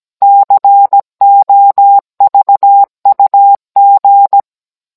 Morse code, heard as a keyed tone, about 17 wpm, 800 Hz. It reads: COVUG
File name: Cours-NR18-16mm